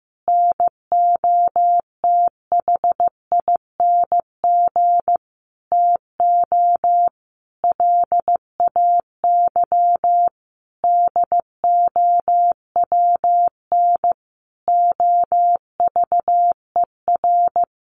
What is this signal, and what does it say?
Morse code, heard as a keyed tone, about 15 wpm, 700 Hz. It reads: NOTHING TO LAY DOWN OVER
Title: Sentences from Top 800 Words 15wpm